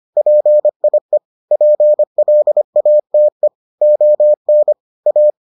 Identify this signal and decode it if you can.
Morse code, heard as a keyed tone, about 25 wpm, 600 Hz. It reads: PIE PLATE ON A